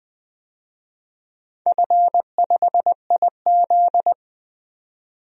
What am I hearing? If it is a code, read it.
Morse code, heard as a keyed tone, about 20 wpm, 700 Hz. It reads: F5IZ
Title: DXCC-100 Call Signs - 4 Characters - 20wpm